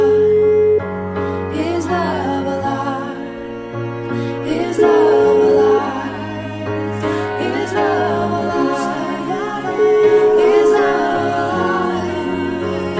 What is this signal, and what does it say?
Ringback tone (Europe, 425 Hz), with noise.